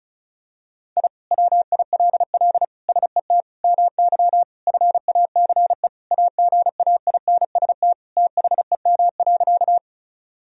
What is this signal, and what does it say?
Morse code, heard as a keyed tone, about 34 wpm, 700 Hz. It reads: I WILL SET MY FACE AGAINST THEM.
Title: Sentences from Top 1000 Words - Review Part 2 - 35wpm